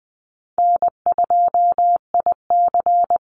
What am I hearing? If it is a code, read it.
Morse code, heard as a keyed tone, about 20 wpm, 700 Hz. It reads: N2IC
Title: Easy Call Signs - 20wpm